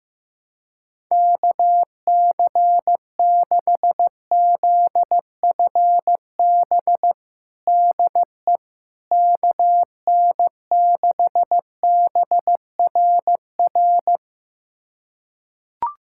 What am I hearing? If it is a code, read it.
Morse code, heard as a keyed tone, about 15 wpm, 700 Hz. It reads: KC6ZFB DE KN6BRR
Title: USA Callsign DE Callsign 15wpm